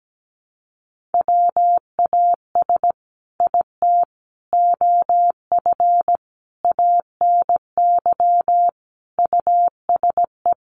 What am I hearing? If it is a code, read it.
Morse code, heard as a keyed tone, about 17 wpm, 700 Hz. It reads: WAS IT OF ANY USE
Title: Sentences from Top 1100 Words 17wpm